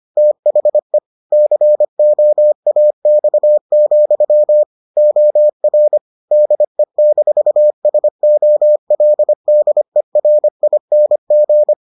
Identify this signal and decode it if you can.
Morse code, heard as a keyed tone, about 25 wpm, 600 Hz. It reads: THE COAX, OR DE-SOLDERING